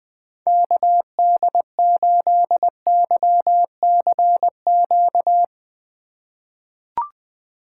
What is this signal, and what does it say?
Morse code, heard as a keyed tone, about 21 wpm, 700 Hz. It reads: KD8YCQ
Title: USA Call Signs - 2x Character Spacing - 20wpm